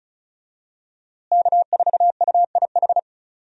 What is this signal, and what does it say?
Morse code, heard as a keyed tone, about 34 wpm, 700 Hz. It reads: K4UIH
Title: USA Call Signs - 5 Characters - Repeated 2x - 35wpm